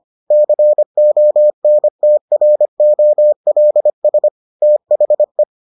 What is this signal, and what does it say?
Morse code, heard as a keyed tone, about 25 wpm, 600 Hz. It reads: CONTROLS THE